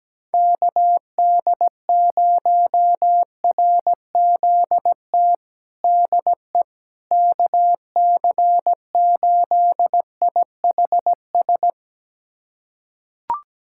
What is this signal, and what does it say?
Morse code, heard as a keyed tone, about 17 wpm, 700 Hz. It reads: KD0RZT DE KC8IHS
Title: USA Callsign DE Callsign 17wpm